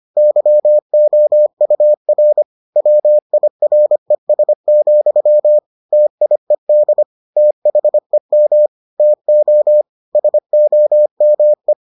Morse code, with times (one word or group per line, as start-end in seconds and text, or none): 0.2-2.4 YOUR
2.8-5.6 WIRES,
5.9-7.0 TIED
7.4-8.7 THEM
9.0-9.8 TO
10.2-11.7 SOME